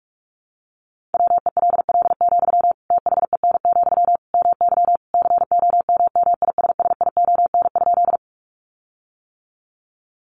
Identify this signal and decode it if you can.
Morse code, heard as a keyed tone, about 44 wpm, 700 Hz. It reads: WELL, THEN, MY COMMISSION?
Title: Sentences from Top 1400 Words 45wpm